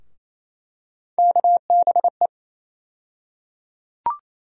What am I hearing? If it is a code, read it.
Morse code, heard as a keyed tone, about 27 wpm, 700 Hz. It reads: KBE
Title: USA Call Sign Postfixes - 3 Letters 28wpm